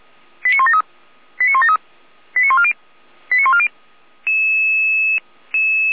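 Five-tone selective call (ZVEI), heard as quick runs of five tones, tones of about 70 ms. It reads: 89173 89173 89140 89140
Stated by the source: ZVEI five-tone selective call received on SDR.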